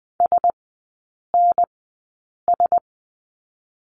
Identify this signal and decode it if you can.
Morse code, heard as a keyed tone, about 20 wpm, spaced as for 8 wpm, 700 Hz. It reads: SNS
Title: 035 Lesson 7e - Letter s - Three Characters 20wpm